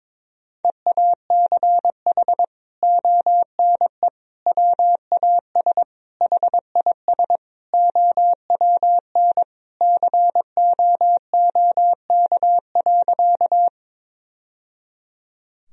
Morse code, heard as a keyed tone, about 22 wpm, 700 Hz. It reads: EACH ONE WAS HIS OWN COOK.
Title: Sentences from Top 1000 Words - Review Part 4 - 22wpm